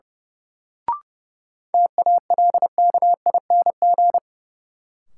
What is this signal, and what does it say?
Morse code, heard as a keyed tone, about 32 wpm, 700 Hz. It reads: TALKING
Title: Sets of 2 Words - New Word Drill - 30wpm